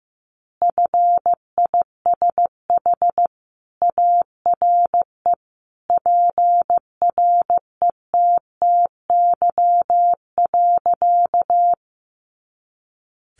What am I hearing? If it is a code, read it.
Morse code, heard as a keyed tone, about 15 wpm, 700 Hz. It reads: FISH ARE PRETTY.
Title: Additional Easy Phrases 15wpm